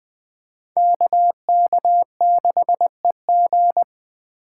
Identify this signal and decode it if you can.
Morse code, heard as a keyed tone, about 21 wpm, 700 Hz. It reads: KK6EG